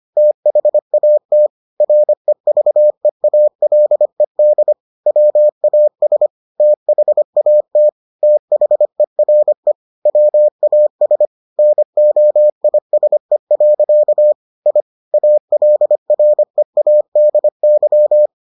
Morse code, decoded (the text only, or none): THAT REVEALED WAS THAT THERE WAS NOISE. I ALREADY